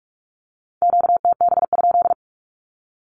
Morse code, heard as a keyed tone, about 44 wpm, 700 Hz. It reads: QTB?
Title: Q-Codes Repeated 3x 45wpm